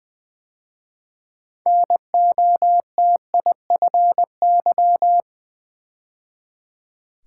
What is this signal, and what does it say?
Morse code, heard as a keyed tone, about 20 wpm, 700 Hz. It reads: NOTIFY
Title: 6 Letter Words - First 3 Letters Given - 20wpm